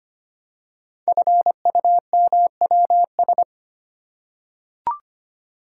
Morse code, heard as a keyed tone, about 25 wpm, 700 Hz. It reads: FUMWS
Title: Code Groups - 5 Letters - 25wpm